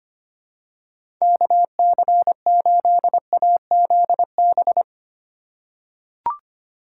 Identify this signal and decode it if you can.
Morse code, heard as a keyed tone, about 25 wpm, 700 Hz. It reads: KC8AZB
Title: USA Call Signs - 6 Characters - 3x Character Spacing - 25wpm